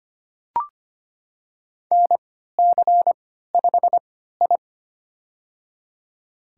Morse code, heard as a keyed tone, about 25 wpm, spaced as for 14 wpm, 700 Hz. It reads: NC5I
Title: USA Call Signs - 4 Characters - 3x Character Spacing - 25wpm